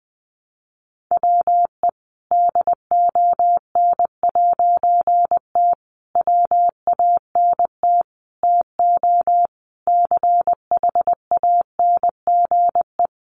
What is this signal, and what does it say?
Morse code, heard as a keyed tone, about 20 wpm, 700 Hz. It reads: WE DON'T WANT TO CHANGE
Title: A Sound Of Thunder 20wpm